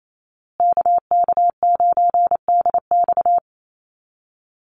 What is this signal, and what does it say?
Morse code, heard as a keyed tone, about 28 wpm, 700 Hz. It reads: KK9DX
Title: Call Signs Repeated 2x - 28wpm